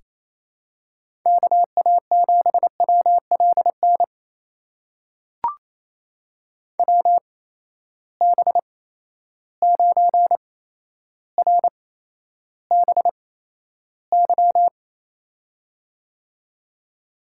Morse code, heard as a keyed tone, about 28 wpm, spaced as for 12 wpm, 700 Hz. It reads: KA7WLN WB9RBY
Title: USA Call Signs - 6 Characters - 8x Character Spacing - 28wpm